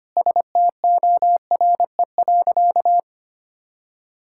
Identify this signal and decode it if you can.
Morse code, heard as a keyed tone, about 25 wpm, 700 Hz. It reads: STORE.
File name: practice / 25wpm / Sentences with 3-Word Phrases - 25wpm